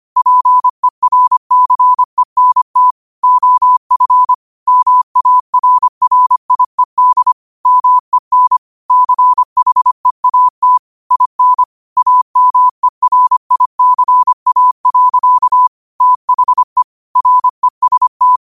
Morse code, heard as a keyed tone, about 25 wpm, 1000 Hz. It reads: PERCENT OF MARRIED MEN CHEAT IN AMERICA. THE REST
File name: qotdfiles / 25wpm